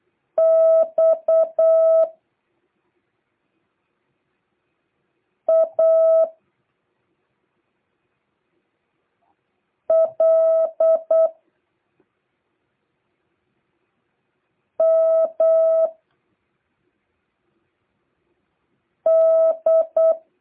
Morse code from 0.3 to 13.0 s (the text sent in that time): X A L